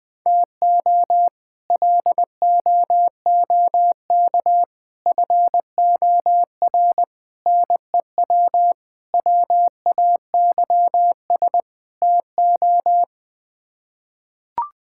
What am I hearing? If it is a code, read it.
Morse code, heard as a keyed tone, about 21 wpm, 700 Hz. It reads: TO LOOK FOR NEW WAYS TO
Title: Sets of 6 Words - 20wpm